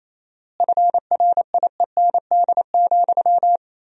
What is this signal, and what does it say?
Morse code, heard as a keyed tone, about 28 wpm, 700 Hz. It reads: FRIEND,
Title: 4x Word-Spacing - Sentences from Top 500 Words - 28wpm